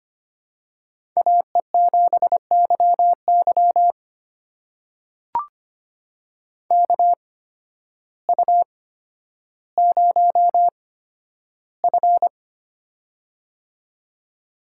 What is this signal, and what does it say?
Morse code, heard as a keyed tone, about 25 wpm, spaced as for 12 wpm, 700 Hz. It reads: AE7YY KU0F
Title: USA Call Signs - 8x Character Spacing - 25wpm